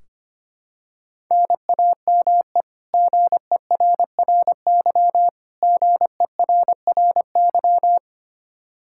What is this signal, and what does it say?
Morse code, heard as a keyed tone, about 25 wpm, 700 Hz. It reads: NAME GERRY GERRY
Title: Practice QSO - Exchange 1 - Beginning - 25wpm